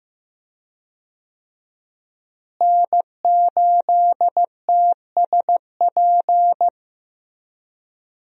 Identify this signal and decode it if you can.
Morse code, heard as a keyed tone, about 15 wpm, 700 Hz. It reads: N8TSP
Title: USA Call Signs - 5 Characters - Repeated 2x - 15wpm